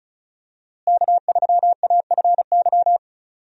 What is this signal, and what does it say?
Morse code, heard as a keyed tone, about 34 wpm, 700 Hz. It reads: K3AFY
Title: USA Call Signs - 5 Characters - Repeated 2x - 35wpm